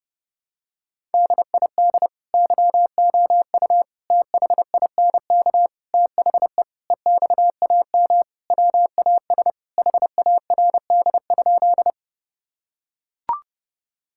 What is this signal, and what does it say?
Morse code, heard as a keyed tone, about 32 wpm, 700 Hz. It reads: DID YOU THINK THE EXAM WAS HARD?
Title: Sentences from Top 1000 Words - Review Part 5 - 30wpm